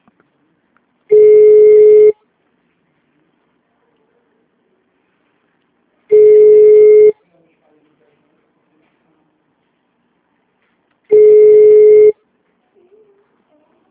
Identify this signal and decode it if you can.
Ringback tone (Europe, 425 Hz).
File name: out- -2001- - - .53624